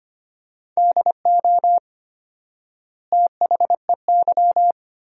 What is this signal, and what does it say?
Morse code, heard as a keyed tone, about 25 wpm, 700 Hz. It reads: DO THEY
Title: 4x Word-Spacing - Sentences from Top 100 Words - 25wpm